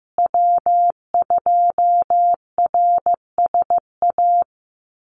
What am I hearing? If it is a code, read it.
Morse code, heard as a keyed tone, about 15 wpm, 700 Hz. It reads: W2RSA
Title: USA Call Signs Repeated 3x 15wpm